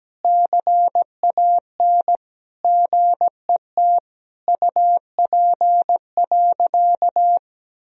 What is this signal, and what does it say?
Morse code, heard as a keyed tone, about 17 wpm, 700 Hz. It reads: CAN GET UP.
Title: Sentences from Top 1000 Words - Review Part 3 - 17wpm